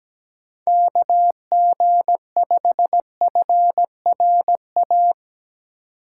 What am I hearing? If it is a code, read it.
Morse code, heard as a keyed tone, about 17 wpm, 700 Hz. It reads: KG5FRA
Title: USA Call Signs - 6 Characters - 3x Character Spacing - 17wpm